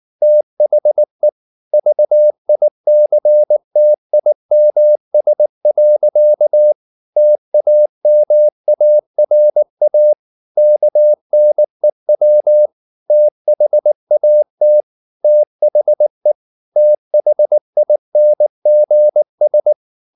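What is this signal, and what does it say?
Morse code, heard as a keyed tone, about 19 wpm, 600 Hz. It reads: THE VICTIMS. TAMARA KNEW THAT THE THINGS
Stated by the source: Morse code at 19 wpm from CW med Gnister, reading Ken Follett's novel Never in English.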